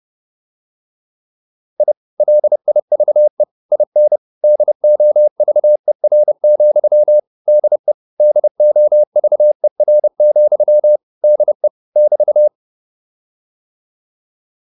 Morse code, heard as a keyed tone, about 30 wpm, 600 Hz. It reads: I LIVE IN DOVER, DE DOVER, DE <BT>